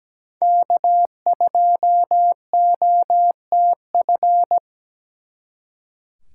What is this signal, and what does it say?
Morse code, heard as a keyed tone, about 20 wpm, 700 Hz. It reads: K2OTF
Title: Speed Racing - USA 1x3 Call Signs 17wpm